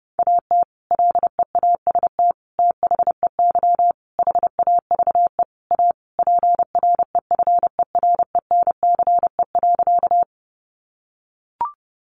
Morse code, heard as a keyed tone, about 30 wpm, 700 Hz. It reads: AT LEAST THEY HAVE A PREFERENCE.
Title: Sentences from Top 1500 Words 30wpm